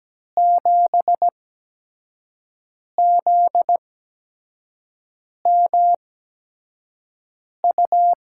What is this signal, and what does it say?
Morse code, heard as a keyed tone, about 17 wpm, spaced as for 5 wpm, 700 Hz. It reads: 7ZMU
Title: USA Call Signs - 6 Characters - 8x Character Spacing - 17wpm